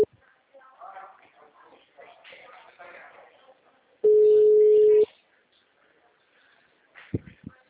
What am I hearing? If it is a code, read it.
Ringback tone (Europe, 425 Hz), with speech.